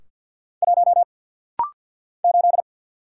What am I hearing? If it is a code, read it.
Morse code, heard as a keyed tone, about 50 wpm, 700 Hz. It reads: 1 8
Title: Single Letter-Number - Rapid-Fire 50wpm